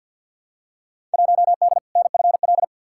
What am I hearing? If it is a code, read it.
Morse code, heard as a keyed tone, about 47 wpm, 700 Hz. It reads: 1D NFL
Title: Field Day Exchanges - 50wpm